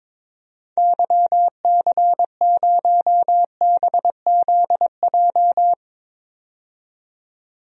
Morse code, heard as a keyed tone, about 22 wpm, 700 Hz. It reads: YC0BZJ
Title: DXCC-100 Call Signs - 6 Characters - 22wpm